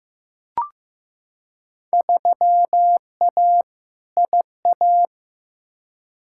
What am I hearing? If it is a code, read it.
Morse code, heard as a keyed tone, about 15 wpm, 700 Hz. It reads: 3A IA
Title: Field Day Exchanges Encore - 15wpm